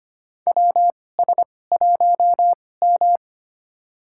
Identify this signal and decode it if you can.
Morse code, heard as a keyed tone, about 25 wpm, spaced as for 18 wpm, 700 Hz. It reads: WS1M
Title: USA Call Signs - 4 Characters - 2x Character Spacing - 25wpm